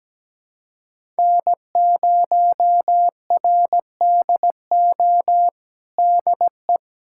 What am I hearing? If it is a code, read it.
Morse code, heard as a keyed tone, about 17 wpm, 700 Hz. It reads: N0RDO DE
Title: USA Callsign DE Callsign - 5 Characters - 17wpm